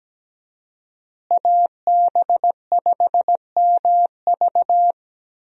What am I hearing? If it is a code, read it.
Morse code, heard as a keyed tone, about 17 wpm, 700 Hz. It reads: AB5MV
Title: USA Call Signs - 5 Characters - Repeated 3x - 17wpm